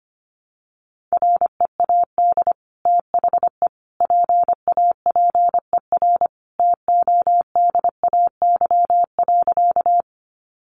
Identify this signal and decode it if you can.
Morse code, heard as a keyed tone, about 25 wpm, 700 Hz. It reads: READ THE PAPER TODAY.
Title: Sentences from Top 600 Words 25wpm